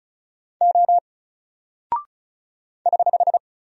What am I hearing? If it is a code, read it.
Morse code, heard as a keyed tone, about 36 wpm, 700 Hz. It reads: O <HH>
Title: Single Character (Letter:Number:Punctuation:Prosign) Alt 35wpm